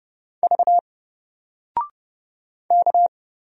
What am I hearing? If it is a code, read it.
Morse code, heard as a keyed tone, about 30 wpm, 700 Hz. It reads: V K
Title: Single Letter-Number Alt 30wpm